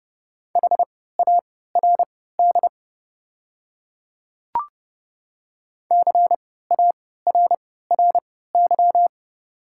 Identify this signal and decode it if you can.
Morse code, heard as a keyed tone, about 30 wpm, spaced as for 17 wpm, 700 Hz. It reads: HARD CARRY
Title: 3x Character Spacing - Top 300 Words - 30wpm